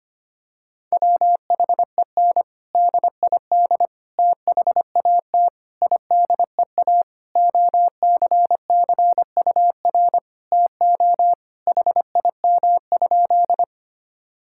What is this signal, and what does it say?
Morse code, heard as a keyed tone, about 25 wpm, 700 Hz. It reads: WHEN DID THAT IDEA OCCUR TO HIM?
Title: Sentences from Top 1000 Words - Review Part 6 - 25wpm